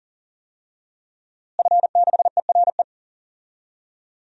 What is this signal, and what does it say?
Morse code, heard as a keyed tone, about 40 wpm, 700 Hz. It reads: F6ERE
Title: DXCC-100 Call Signs - 5 Characters - Repeated 3x - 40wpm